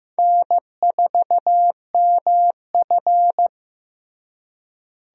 Morse code, heard as a keyed tone, about 15 wpm, 700 Hz. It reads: N4MF
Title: USA Call Signs - 4 Characters - Repeated 3x - 15wpm